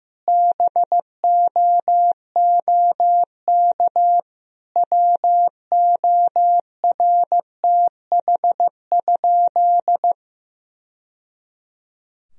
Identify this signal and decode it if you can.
Morse code, heard as a keyed tone, about 15 wpm, 700 Hz. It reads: BOOK WORTH?
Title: Sentences from Top 1000 Words - Review Part 1 - 15wpm